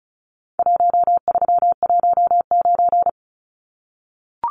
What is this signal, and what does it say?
Morse code, heard as a keyed tone, about 34 wpm, 700 Hz. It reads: 1319
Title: Numbers - Contest Style - 4-Digit 35wpm